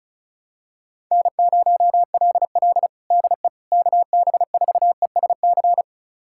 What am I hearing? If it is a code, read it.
Morse code, heard as a keyed tone, about 36 wpm, 700 Hz. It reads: N0LL DE KB4ESC
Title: USA Callsign DE Callsign 35wpm